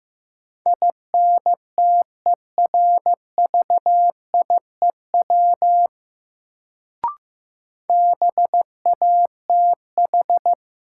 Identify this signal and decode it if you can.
Morse code, heard as a keyed tone, about 15 wpm, 700 Hz. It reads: INTERVIEW BATH
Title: Sentences from Top 1600 Words - New Word Drill Alt - 15wpm